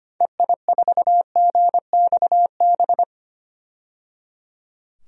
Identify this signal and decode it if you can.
Morse code, heard as a keyed tone, about 25 wpm, 700 Hz. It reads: EI4GXB
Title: DXCC-100 Call Signs - 6 Characters - 25wpm